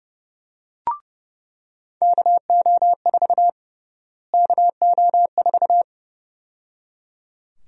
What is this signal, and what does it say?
Morse code, heard as a keyed tone, about 31 wpm, 700 Hz. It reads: KO4 KO4
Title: USA Call Sign Prefixes - Repeated 2x - 30wpm